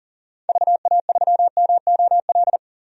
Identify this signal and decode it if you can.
Morse code, heard as a keyed tone, about 40 wpm, 700 Hz. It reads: VA3MOL
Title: DXCC-100 Call Signs - 6 Characters - 40wpm